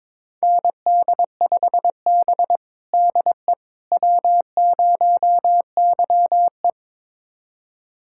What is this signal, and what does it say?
Morse code, heard as a keyed tone, about 21 wpm, 700 Hz. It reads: ND5B DE W0YE
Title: USA Callsign DE Callsign - 4 Characters - 22wpm